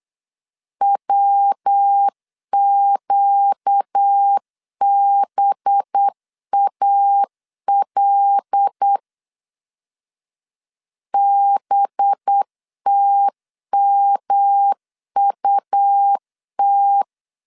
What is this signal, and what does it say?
Morse code, heard as a keyed tone, about 8 wpm, 800 Hz. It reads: WQBAL BTMUT
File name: CW / DET / 1 6 1500C S B 8wpm 5142